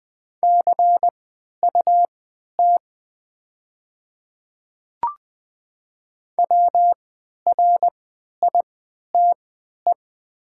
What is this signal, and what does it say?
Morse code, heard as a keyed tone, about 20 wpm, spaced as for 11 wpm, 700 Hz. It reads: CUT WRITE
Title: 3x Character Spacing - Top 300 Words - 20wpm